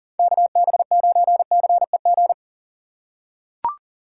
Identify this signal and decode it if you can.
Morse code, heard as a keyed tone, about 40 wpm, 700 Hz. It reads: KB9CEG
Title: USA Call Signs - 6 Characters - 2x Character Spacing - 40wpm